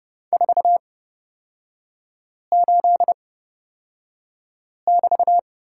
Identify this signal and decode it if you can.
Morse code, heard as a keyed tone, about 30 wpm, spaced as for 5 wpm, 700 Hz. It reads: 48<BT>
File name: Lesson 36e - BT - Three Characters 30wpm